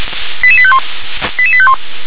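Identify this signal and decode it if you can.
Five-tone selective call (ZVEI), heard as quick runs of five tones, tones of about 70 ms. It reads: 88851 88851
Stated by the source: Received via SDR; ZVEI five-tone call 88911.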